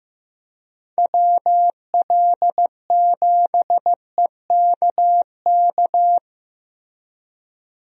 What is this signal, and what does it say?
Morse code, heard as a keyed tone, about 15 wpm, 700 Hz. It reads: WL7EKK
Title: USA Call Signs - 6 Characters - Repeated 3x - 15wpm